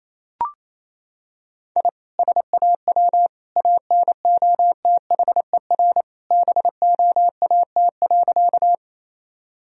Morse code, heard as a keyed tone, about 28 wpm, 700 Hz. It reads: I SAW ANOTHER BOAT.
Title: Sentences from Top 1000 Words - Review Part 3 - 28wpm